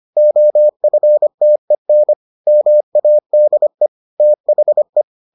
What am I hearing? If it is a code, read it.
Morse code, heard as a keyed tone, about 25 wpm, 600 Hz. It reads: OFTEN MADE THE